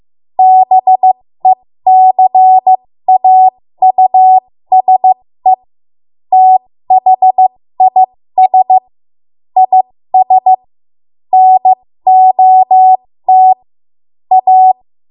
Morse code, heard as a keyed tone, about 15 wpm, spaced as for 13 wpm, 750 Hz. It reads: BECAUSE THIS IS NOT A
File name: Morse / 13WPM